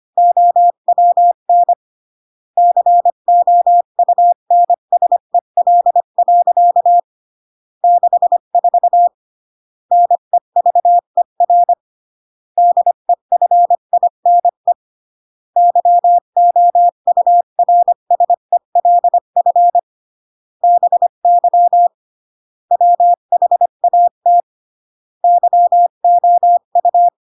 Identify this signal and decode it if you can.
Morse code, heard as a keyed tone, about 25 wpm, 700 Hz. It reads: OWN COUNSEL. 64 NEVER DEFINE YOURSELF BY WHAT YOU